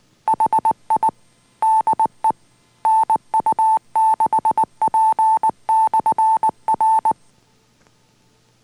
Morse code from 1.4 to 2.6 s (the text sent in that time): DE